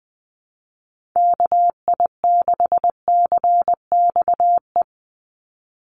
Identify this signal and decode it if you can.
Morse code, heard as a keyed tone, about 20 wpm, 700 Hz. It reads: KI6CXE
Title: USA Call Signs 20wpm